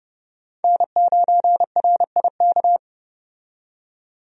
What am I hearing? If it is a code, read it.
Morse code, heard as a keyed tone, about 30 wpm, 700 Hz. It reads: N9RIK
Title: USA Call Signs - 5 Characters - 30wpm